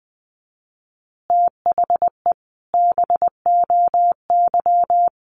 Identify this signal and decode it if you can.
Morse code, heard as a keyed tone, about 20 wpm, 700 Hz. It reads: THE BOY
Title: Easy Phrases 20wpm